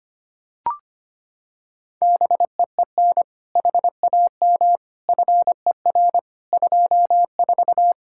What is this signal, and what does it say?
Morse code, heard as a keyed tone, about 25 wpm, 700 Hz. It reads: BEEN HAM FER 24
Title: Practice QSO - Exchange 3 - More Info - 25wpm